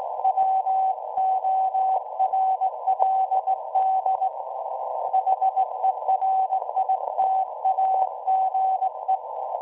Morse code, heard as a keyed tone, about 18 wpm, 760 Hz. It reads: WORLD HERITAGE
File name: WebSDR records / SAQ at PI4THT part od 1005z 17-2kHz